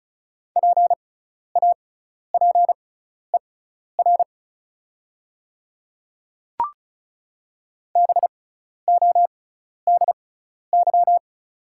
Morse code, heard as a keyed tone, about 36 wpm, spaced as for 12 wpm, 690 Hz. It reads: PAPER BODY
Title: 6x Character Spacing - Top 500 Words - 35wpm